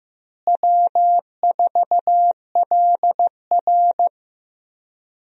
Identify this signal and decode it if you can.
Morse code, heard as a keyed tone, about 15 wpm, 700 Hz. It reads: W4LR
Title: USA Call Signs - 4 Characters - Repeated 2x - 15wpm